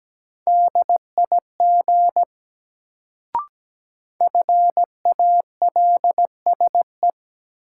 Morse code, heard as a keyed tone, about 17 wpm, 700 Hz. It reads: DIG FALSE
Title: Sentences from Top 1600 Words - New Word Drill - 17wpm